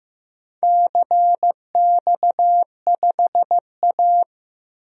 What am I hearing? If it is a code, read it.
Morse code, heard as a keyed tone, about 15 wpm, 700 Hz. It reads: CX5A
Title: DXCC-100 Call Signs - 4 Characters - 15wpm